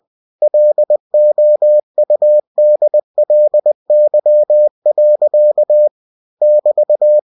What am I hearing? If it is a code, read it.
Morse code, heard as a keyed tone, about 20 wpm, 600 Hz. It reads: LOUDLY. <BT>
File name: cw-books / winnie-the-pooh / winnie-20wpm-page-0013